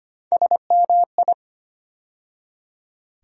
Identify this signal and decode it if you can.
Morse code, heard as a keyed tone, about 25 wpm, 700 Hz. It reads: SMI
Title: USA Call Sign Postfixes - 3 Letters 25wpm